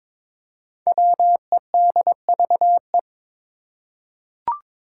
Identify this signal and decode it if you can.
Morse code, heard as a keyed tone, about 22 wpm, 700 Hz. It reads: WEDVE
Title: Five Challenging Letter-Number Combinations 22wpm